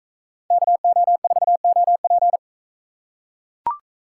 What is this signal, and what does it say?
Morse code, heard as a keyed tone, about 40 wpm, 700 Hz. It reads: KO4OP